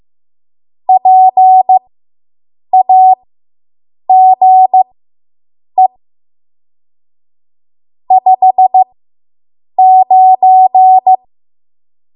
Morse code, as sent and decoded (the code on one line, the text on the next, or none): .--. .- --. / ..... ----.
PAGE 59